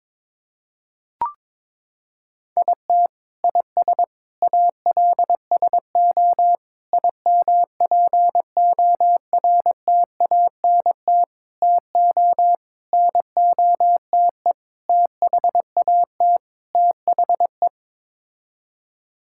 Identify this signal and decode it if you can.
Morse code, heard as a keyed tone, about 21 wpm, 700 Hz. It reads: IT IS ALSO IMPORTANT TO NOTE THAT THE
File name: practice / 22wpm / Sets of 8 Words - 22wpm